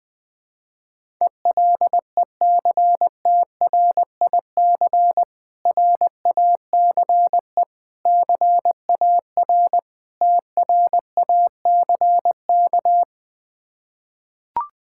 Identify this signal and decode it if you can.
Morse code, heard as a keyed tone, about 21 wpm, 700 Hz. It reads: ELECTRIC RACE CAR TRACK
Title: Difficult Words with Hints - 20wpm